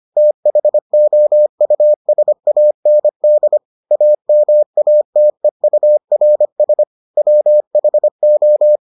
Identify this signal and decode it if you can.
Morse code, heard as a keyed tone, about 25 wpm, 600 Hz. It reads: THOUSAND AMATEURS WHO